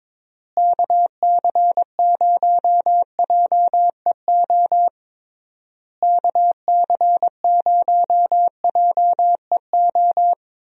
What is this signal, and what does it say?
Morse code, heard as a keyed tone, about 22 wpm, 700 Hz. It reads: KC0JEO KC0JEO
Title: USA Call Signs - 6 Characters - Repeated 3x - 22wpm